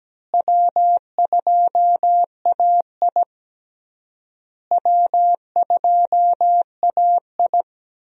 Morse code, heard as a keyed tone, about 17 wpm, 700 Hz. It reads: W2AI W2AI
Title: USA Call Signs - 4 Characters - Repeated 3x - 17wpm